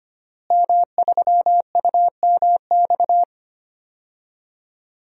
Morse code, heard as a keyed tone, about 25 wpm, 700 Hz. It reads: M3UMX
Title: DXCC-100 Call Signs - 5 Characters - Repeated 3x - 25wpm